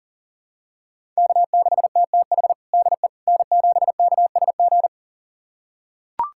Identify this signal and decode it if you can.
Morse code, heard as a keyed tone, about 40 wpm, 700 Hz. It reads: K6TTH DE N7KSG
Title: USA Callsign DE Callsign - 5 Characters - 40wpm